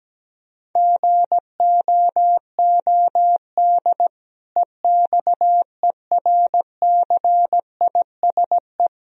Morse code, heard as a keyed tone, about 17 wpm, 700 Hz. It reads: GOOD EXERCISE
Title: Sentences from Top 1000 Words - Review Part 4 - 17wpm